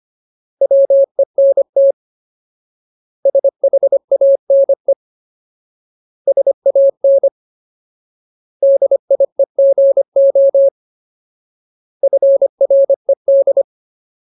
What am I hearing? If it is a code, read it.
Morse code, heard as a keyed tone, about 25 wpm, 550 Hz. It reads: WENT SHANE SAN DIEGO FRED
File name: Ultimate Word List / 25WPM Ultimate